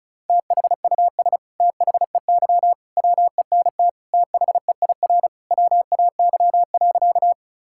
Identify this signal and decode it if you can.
Morse code, heard as a keyed tone, about 36 wpm, 700 Hz. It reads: THUS THEY WENT THEIR WAY.
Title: Sentences from Top 1000 Words - Review Part 5 - 35wpm